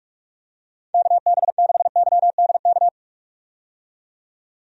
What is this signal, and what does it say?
Morse code, heard as a keyed tone, about 47 wpm, 700 Hz. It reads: KB6YDK
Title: USA Call Signs - 6 Characters - 45wpm